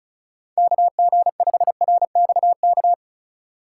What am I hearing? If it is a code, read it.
Morse code, heard as a keyed tone, about 36 wpm, 700 Hz. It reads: KG5RXK
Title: USA Call Signs - 3x Character Spacing - 35wpm